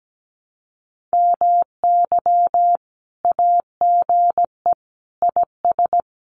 Morse code, heard as a keyed tone, about 17 wpm, 700 Hz. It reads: MY AGE IS
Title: Common QSO Phrases Encore 17wpm